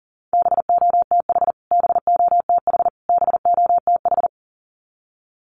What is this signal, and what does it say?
Morse code, heard as a keyed tone, about 40 wpm, 700 Hz. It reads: BOTH BOTH BOTH
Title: Top 300 Words Repeated 40wpm